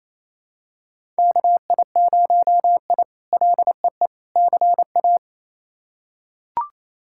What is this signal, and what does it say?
Morse code, heard as a keyed tone, about 27 wpm, 700 Hz. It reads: KI0I LEE CA
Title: K1USN SST QSO Practice - 28wpm